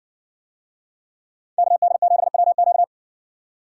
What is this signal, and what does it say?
Morse code, heard as a keyed tone, about 60 wpm, 700 Hz. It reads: KD7WQ